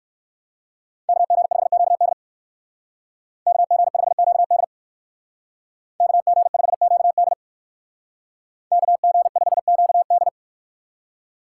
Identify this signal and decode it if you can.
Morse code, heard as a keyed tone, about 54 wpm, 700 Hz. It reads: KG5QD KG5QD KG5QD KG5QD
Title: Speed Racing - USA 2x2 Call Signs 45wpm